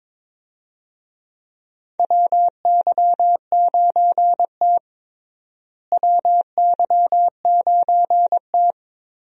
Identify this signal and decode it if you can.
Morse code, heard as a keyed tone, about 22 wpm, 700 Hz. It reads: WY9T WY9T
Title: USA Call Signs - 4 Characters - Repeated 3x - 22wpm